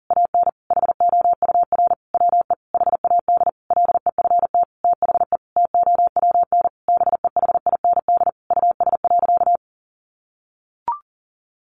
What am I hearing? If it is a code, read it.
Morse code, heard as a keyed tone, about 40 wpm, 700 Hz. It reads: AN HOUR WE HAD LEFT THE TOWN BEHIND US.
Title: Sentences from Top 1300 Words 40wpm